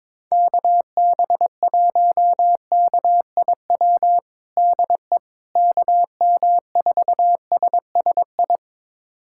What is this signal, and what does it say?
Morse code, heard as a keyed tone, about 22 wpm, 700 Hz. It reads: KB1KIW DE KM4SSI
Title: USA Callsign DE Callsign 22wpm